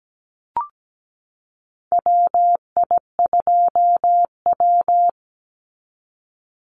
Morse code, heard as a keyed tone, about 17 wpm, 700 Hz. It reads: WI2W
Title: NA SOTA Chaser Callsigns 17wpm